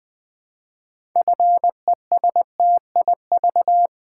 Morse code, heard as a keyed tone, about 20 wpm, 700 Hz. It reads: FESTIV
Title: Build A Word - 2 Letters At A Time - 8-Letter Words - 20wpm